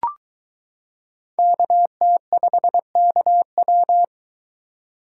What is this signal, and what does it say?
Morse code, heard as a keyed tone, about 22 wpm, 700 Hz. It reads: KT5KW